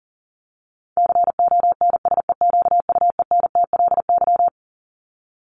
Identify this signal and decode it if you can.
Morse code, heard as a keyed tone, about 40 wpm, 700 Hz. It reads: CONSEQUENTLY
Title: LY Suffix - Top 100 Words with Base Words 40wpm